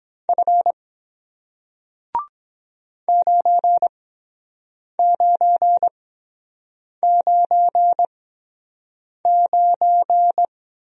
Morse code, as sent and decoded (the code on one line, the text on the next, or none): ..-. / ----. / ----. / ----. / ----.
F 9 9 9 9